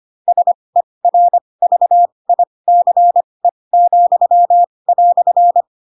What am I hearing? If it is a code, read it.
Morse code, heard as a keyed tone, about 25 wpm, spaced as for 20 wpm, 700 Hz. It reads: SERVICE,"
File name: Pinocchio / Pinocchio ch2 15 25 ik1hge